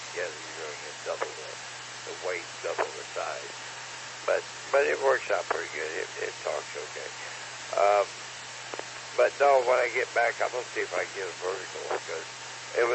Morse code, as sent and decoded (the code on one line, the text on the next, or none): none